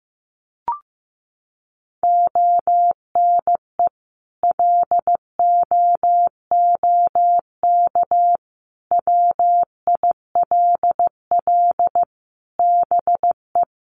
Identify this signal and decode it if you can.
Morse code, heard as a keyed tone, about 15 wpm, 700 Hz. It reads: ONE LOOK WILL BE
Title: Sentences from Top 1400 Words 15wpm